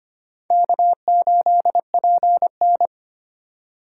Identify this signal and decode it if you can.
Morse code, heard as a keyed tone, about 32 wpm, 700 Hz. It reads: K8PN